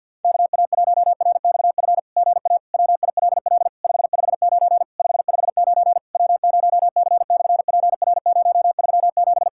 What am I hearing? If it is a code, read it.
Morse code, heard as a keyed tone, about 50 wpm, 700 Hz. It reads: KA1RXF GA WILL 559 559 W0C/PR037
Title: SOTA QSO Practice 50wpm